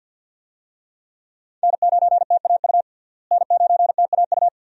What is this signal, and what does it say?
Morse code, heard as a keyed tone, about 52 wpm, 700 Hz. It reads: N9TAU N9TAU
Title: USA Call Signs - 5 Characters - Repeated 2x - 50wpm